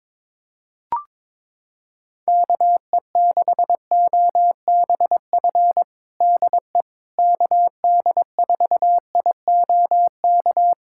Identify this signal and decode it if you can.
Morse code, heard as a keyed tone, about 22 wpm, 700 Hz. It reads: KE6OBF DE KD4IOK
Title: USA Callsign DE Callsign 22wpm